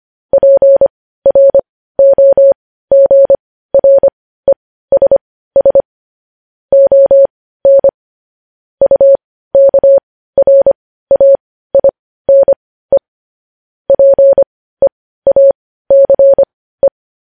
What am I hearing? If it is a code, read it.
Morse code, heard as a keyed tone, about 25 wpm, spaced as for 15 wpm, 550 Hz. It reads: PROGRESS ON UKRAINE PEACE
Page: news-headlines-in-morse-code-at-15-wpm